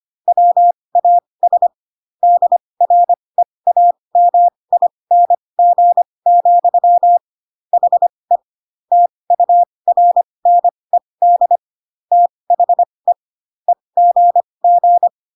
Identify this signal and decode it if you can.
Morse code, heard as a keyed tone, about 25 wpm, spaced as for 20 wpm, 700 Hz. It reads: WAS DREAMING, HE TURNED THE EGG